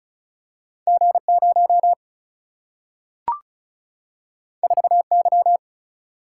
Morse code, heard as a keyed tone, about 36 wpm, 700 Hz. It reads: G0 4Y
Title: Single-Letter Single-Number 35wpm